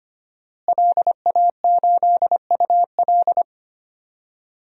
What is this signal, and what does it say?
Morse code, heard as a keyed tone, about 25 wpm, 700 Hz. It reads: LA8UL
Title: DXCC-100 Call Signs - 5 Characters - Repeated 2x - 25wpm